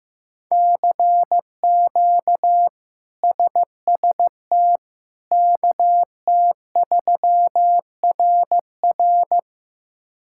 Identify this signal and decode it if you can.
Morse code, heard as a keyed tone, about 16 wpm, 700 Hz. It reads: CQ SST KT3RR
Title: K1USN SST QSO Practice - 15wpm